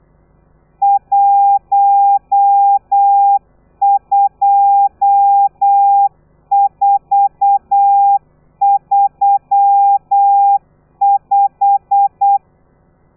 Morse code, heard as a keyed tone, about 8 wpm, 800 Hz. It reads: 12435